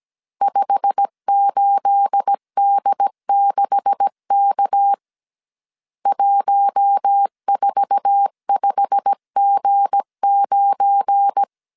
Morse code, heard as a keyed tone, about 17 wpm, 800 Hz. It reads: 58D6K 145G9
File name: CW / DET / 1 9 1500C S C 16wpm 2909